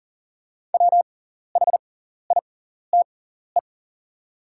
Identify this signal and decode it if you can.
Morse code, heard as a keyed tone, about 40 wpm, spaced as for 14 wpm, 690 Hz. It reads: WHITE
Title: 6x Character Spacing - Top 300 Words - 40wpm